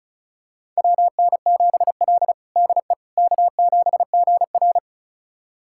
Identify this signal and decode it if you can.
Morse code, heard as a keyed tone, about 36 wpm, 700 Hz. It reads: WN7L DE K7GR
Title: USA Callsign DE Callsign - 4 Characters - 35wpm